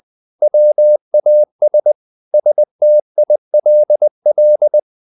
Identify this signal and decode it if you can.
Morse code, heard as a keyed tone, about 20 wpm, 600 Hz. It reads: WAS STILL
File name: the old man and the sea / the old man and the sea part 1 20wpm